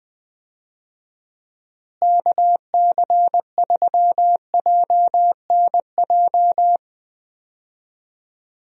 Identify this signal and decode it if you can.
Morse code, heard as a keyed tone, about 21 wpm, 700 Hz. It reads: KC3JNJ
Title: USA Call Signs - 6 Characters - Repeated 2x - 20wpm